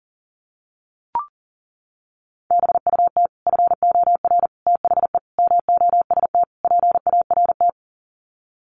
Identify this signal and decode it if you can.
Morse code, heard as a keyed tone, about 40 wpm, 700 Hz. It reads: BUT FOR THE MOST PART
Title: Sets of 5 Words Encore 40wpm